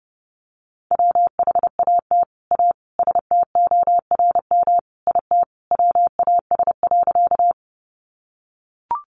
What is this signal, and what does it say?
Morse code, heard as a keyed tone, about 28 wpm, 700 Hz. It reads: WHAT A STORM IT WAS.
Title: Sentences from Top 1500 Words 30wpm